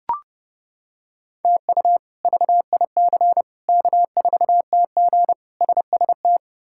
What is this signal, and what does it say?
Morse code, heard as a keyed tone, about 31 wpm, 700 Hz. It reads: TU VIC K4TG SST
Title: K1USN SST QSO Practice - 30wpm